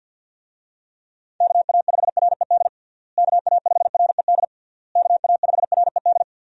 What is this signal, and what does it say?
Morse code, heard as a keyed tone, about 46 wpm, 700 Hz. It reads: KA5RED KA5RED KA5RED
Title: USA Call Signs - 6 Characters - Repeated 3x - 50wpm